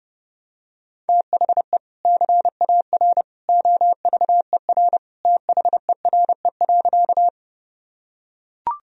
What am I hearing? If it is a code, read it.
Morse code, heard as a keyed tone, about 30 wpm, 700 Hz. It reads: THE CAR OVER THERE.
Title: Sentences from Top 1000 Words - Review Part 1 - 30wpm